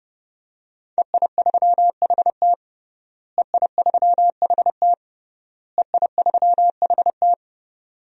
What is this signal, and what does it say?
Morse code, heard as a keyed tone, about 30 wpm, 700 Hz. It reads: EI3HT EI3HT EI3HT
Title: DXCC-100 Call Signs - 5 Characters - Repeated 3x - 30wpm